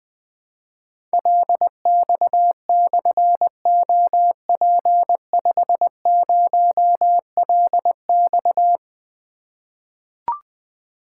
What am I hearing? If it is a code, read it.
Morse code, heard as a keyed tone, about 20 wpm, 700 Hz. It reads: LX/OP50LX
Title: Difficult Call Signs 20wpm